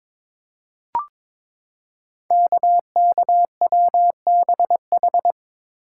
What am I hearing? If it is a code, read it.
Morse code, heard as a keyed tone, about 22 wpm, 700 Hz. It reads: KKWBH
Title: Code Groups - 5 Letters - 22wpm